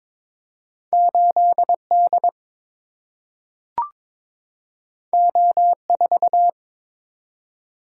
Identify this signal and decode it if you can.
Morse code, heard as a keyed tone, about 21 wpm, 700 Hz. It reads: 8D O4
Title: Single-Letter Single-Number 22wpm